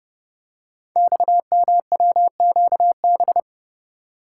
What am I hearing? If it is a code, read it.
Morse code, heard as a keyed tone, about 30 wpm, 700 Hz. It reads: XMWQB
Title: Code Groups - 5 Letter:Numbers - 30wpm